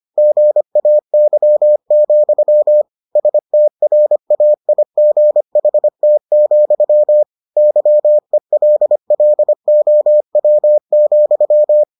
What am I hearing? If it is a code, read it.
Morse code, heard as a keyed tone, about 25 wpm, 600 Hz. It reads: GAY, STRAIGHT, YELLOW,